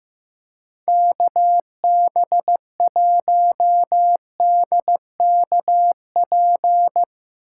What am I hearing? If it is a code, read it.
Morse code, heard as a keyed tone, about 15 wpm, 700 Hz. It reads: KB1DKP
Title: USA Call Signs - 6 Characters - 2x Character Spacing - 15wpm